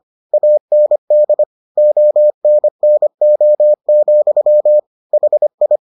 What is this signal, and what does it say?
Morse code, heard as a keyed tone, about 25 wpm, 600 Hz. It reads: AND ONNO, HI